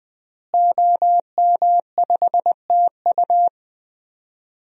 Morse code, heard as a keyed tone, about 20 wpm, 700 Hz. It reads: OM5TU
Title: DXCC-100 Call Signs - 5 Characters - Repeated 3x - 20wpm